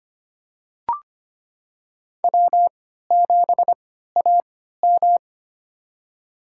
Morse code, heard as a keyed tone, about 25 wpm, spaced as for 14 wpm, 700 Hz. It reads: W7AM
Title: USA Call Signs - 4 Characters - 3x Character Spacing - 25wpm